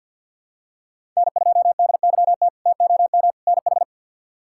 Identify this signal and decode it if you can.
Morse code, heard as a keyed tone, about 46 wpm, 700 Hz. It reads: N2DYT TOM NH
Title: K1USN SST QSO Practice - 50wpm